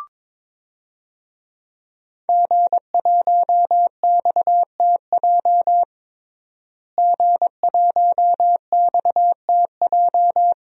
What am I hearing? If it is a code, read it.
Morse code, heard as a keyed tone, about 22 wpm, 700 Hz. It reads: G1XTJ G1XTJ
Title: DXCC-100 Call Signs - 5 Characters - Repeated 3x - 22wpm